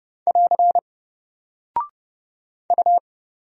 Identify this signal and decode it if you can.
Morse code, heard as a keyed tone, about 30 wpm, 700 Hz. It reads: <AR> U
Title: Single Character (Letter:Number:Punctuation:Prosign) 30wpm